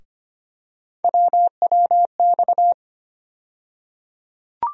Morse code, heard as a keyed tone, about 25 wpm, 700 Hz. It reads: WWX
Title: USA Call Sign Postfixes - 3 Letters 25wpm